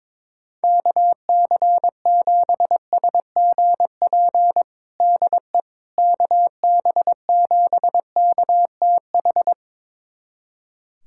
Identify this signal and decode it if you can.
Morse code, heard as a keyed tone, about 22 wpm, 700 Hz. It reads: KC7SGP DE KB7KTH
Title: USA Callsign DE Callsign 22wpm